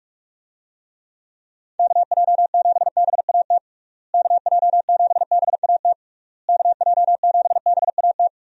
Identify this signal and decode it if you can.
Morse code, heard as a keyed tone, about 46 wpm, 700 Hz. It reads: KJ7BAT KJ7BAT KJ7BAT
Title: USA Call Signs - 6 Characters - Repeated 3x - 45wpm